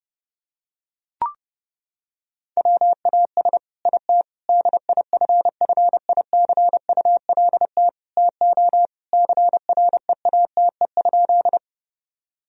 Morse code, heard as a keyed tone, about 32 wpm, 700 Hz. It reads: WAS IT DIFFICULT TO CREATE?
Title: Sentences from Top 1000 Words - Review Part 3 - 30wpm